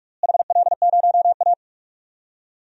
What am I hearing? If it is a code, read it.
Morse code, heard as a keyed tone, about 43 wpm, 700 Hz. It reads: HL0A
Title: DXCC-100 Call Signs - 4 Characters - Repeated 2x - 45wpm